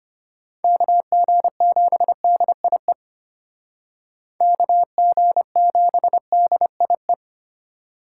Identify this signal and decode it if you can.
Morse code, heard as a keyed tone, about 33 wpm, 700 Hz. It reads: KG7DIE KG7DIE